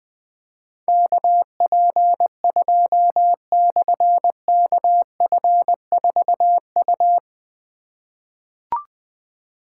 Morse code, heard as a keyed tone, about 20 wpm, 700 Hz. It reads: KP2/KF4U
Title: Difficult Call Signs 20wpm